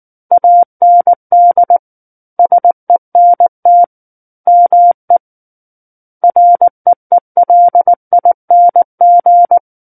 Morse code, heard as a keyed tone, about 19 wpm, 700 Hz. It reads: AND SENT ME REELING
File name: Chapter 10 - 19 WPM